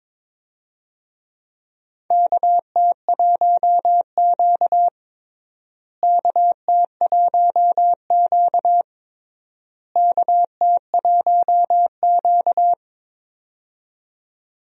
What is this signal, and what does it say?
Morse code, heard as a keyed tone, about 21 wpm, 700 Hz. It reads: KT1Q KT1Q KT1Q
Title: USA Call Signs - 4 Characters - Repeated 3x - 22wpm